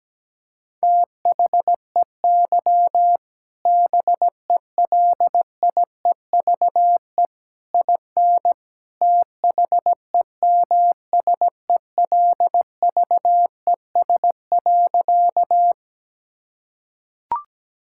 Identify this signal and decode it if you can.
Morse code, heard as a keyed tone, about 17 wpm, 700 Hz. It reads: THEY BELIEVE IN THEMSELVES.
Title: Sentences from Top 1000 Words - Review Part 5 - 17wpm